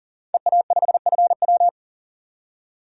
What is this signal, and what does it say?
Morse code, heard as a keyed tone, about 40 wpm, 700 Hz. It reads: EA5FW
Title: DXCC-100 Call Signs - 5 Characters - 40wpm